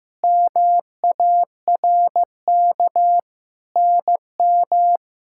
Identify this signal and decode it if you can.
Morse code, heard as a keyed tone, about 15 wpm, 700 Hz. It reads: MARK NM
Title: K1USN SST QSO Practice - 15wpm